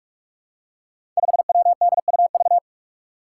Morse code, heard as a keyed tone, about 44 wpm, 700 Hz. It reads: 5WDUV
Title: Five Challenging Letter-Number Combinations 45wpm